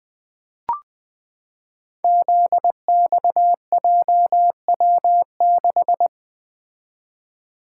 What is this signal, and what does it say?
Morse code, heard as a keyed tone, about 20 wpm, 700 Hz. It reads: ZXJW6
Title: Code Groups - 5 Letter:Numbers - 20wpm